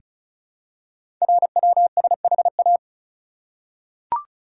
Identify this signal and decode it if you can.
Morse code, heard as a keyed tone, about 36 wpm, 700 Hz. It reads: RWSHA
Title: Code Groups - 5 Letters - 35wpm